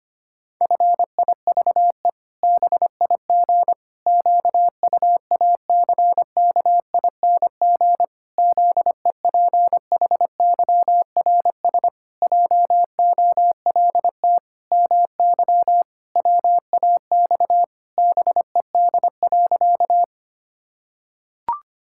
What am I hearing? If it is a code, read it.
Morse code, heard as a keyed tone, about 25 wpm, 700 Hz. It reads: FIVE BIG QUACKING ZEPHYRS JOLT MY WAX BED.
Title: Pangram Challenge v2 - 25wpm